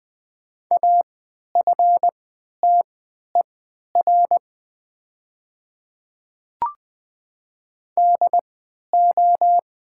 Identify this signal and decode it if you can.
Morse code, heard as a keyed tone, about 20 wpm, spaced as for 11 wpm, 700 Hz. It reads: AFTER DO
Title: 3x Character Spacing - Top 500 Words - 20wpm